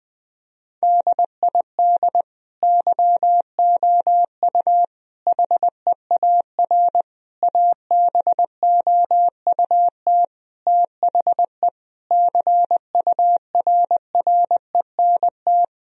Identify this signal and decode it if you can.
Morse code, heard as a keyed tone, about 20 wpm, 700 Hz. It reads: DID YOU HEAR ABOUT THE CURRENT
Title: Sentences from Top 1000 Words - Review Part 4 - 20wpm